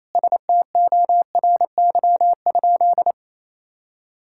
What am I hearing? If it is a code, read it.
Morse code, heard as a keyed tone, about 27 wpm, 700 Hz. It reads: STORY?
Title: Sentences from Top 1000 Words - Review Part 5 - 28wpm